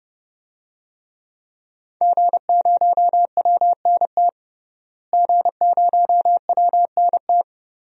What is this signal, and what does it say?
Morse code, heard as a keyed tone, about 30 wpm, 700 Hz. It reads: G0WNT G0WNT
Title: DXCC-100 Call Signs - 5 Characters - Repeated 2x - 30wpm